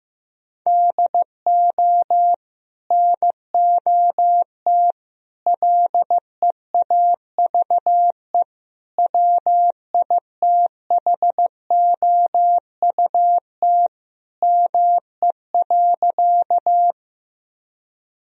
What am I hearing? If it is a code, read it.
Morse code, heard as a keyed tone, about 16 wpm, 700 Hz. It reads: DO NOT LEAVE WITHOUT ME.
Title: Sentences from Top 1000 Words - Review Part 3 - 15wpm